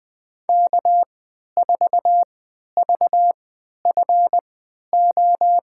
Morse code, heard as a keyed tone, about 20 wpm, spaced as for 11 wpm, 700 Hz. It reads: K4VFO
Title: USA Call Signs - 3x Character Spacing - 20wpm